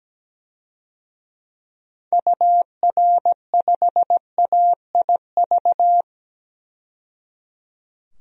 Morse code, heard as a keyed tone, about 17 wpm, 700 Hz. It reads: UR5AIV
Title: DXCC-100 Call Signs - 6 Characters - 17wpm